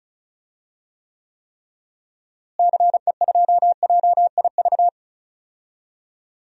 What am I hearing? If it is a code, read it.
Morse code, heard as a keyed tone, about 36 wpm, 700 Hz. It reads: CE2JIV
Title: DXCC-100 Call Signs - 6 Characters - Repeated 3x - 35wpm